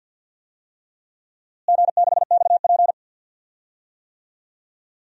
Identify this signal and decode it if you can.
Morse code, heard as a keyed tone, about 49 wpm, 700 Hz. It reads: G6XP
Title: DXCC-100 Call Signs - 4 Characters - Repeated 3x - 50wpm